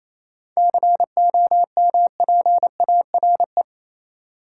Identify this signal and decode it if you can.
Morse code, heard as a keyed tone, about 28 wpm, 700 Hz. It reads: COMPARE
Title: Sentences from Top 1300 Words - New Word Drill - 28wpm